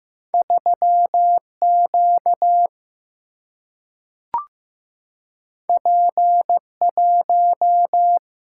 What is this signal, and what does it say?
Morse code, heard as a keyed tone, about 15 wpm, 700 Hz. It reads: 3 Q P 1
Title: Single-Letter Single-Number 15wpm